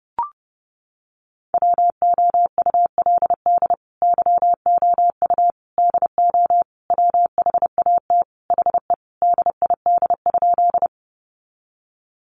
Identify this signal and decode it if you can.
Morse code, heard as a keyed tone, about 28 wpm, 700 Hz. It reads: WOULD YOU DO WHAT HE DID?
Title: Sentences from Top 100 Words 30wpm